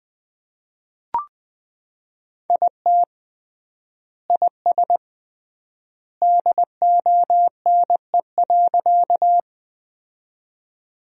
Morse code, heard as a keyed tone, about 21 wpm, 700 Hz. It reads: IT IS DONE.
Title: 3x Word-Spacing - Sentences from Top 500 Words - 20wpm